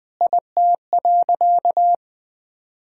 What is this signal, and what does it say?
Morse code, heard as a keyed tone, about 20 wpm, 700 Hz. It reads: IT.
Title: Sentences with 7-Word Phrases - 20wpm